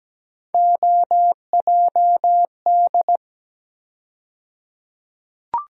Morse code, heard as a keyed tone, about 17 wpm, 700 Hz. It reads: OJD
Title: USA Call Sign Postfixes - 3 Letters 17wpm